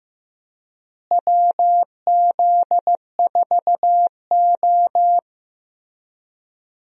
Morse code, heard as a keyed tone, about 15 wpm, 700 Hz. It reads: WZ4O
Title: USA Call Signs - 4 Characters - Repeated 2x - 15wpm